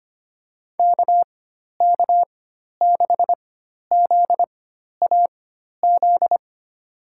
Morse code, heard as a keyed tone, about 25 wpm, spaced as for 12 wpm, 700 Hz. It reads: KK6ZAZ
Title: USA Call Signs - 6 Characters - 4x Character Spacing - 25wpm